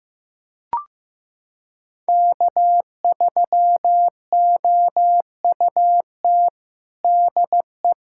Morse code, heard as a keyed tone, about 15 wpm, 700 Hz. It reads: K3OUT DE
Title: USA Callsign DE Callsign - 5 Characters - 15wpm